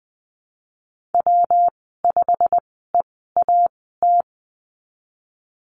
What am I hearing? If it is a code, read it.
Morse code, heard as a keyed tone, about 20 wpm, spaced as for 15 wpm, 700 Hz. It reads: W5EAT
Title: 179 Lesson 29g - Slash - Call Signs 20wpm